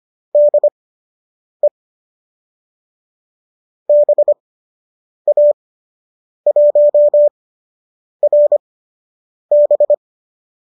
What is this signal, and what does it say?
Morse code, heard as a keyed tone, about 25 wpm, spaced as for 8 wpm, 590 Hz. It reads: DE BA1RB